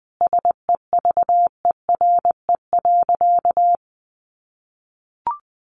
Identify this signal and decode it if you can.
Morse code, heard as a keyed tone, about 20 wpm, 700 Hz. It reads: SEVERE.
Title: Sentences from Top 1400 Words 20wpm